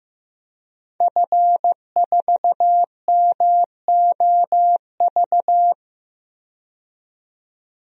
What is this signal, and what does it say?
Morse code, heard as a keyed tone, about 15 wpm, 700 Hz. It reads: F4MOV
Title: DXCC-100 Call Signs - 5 Characters - 15wpm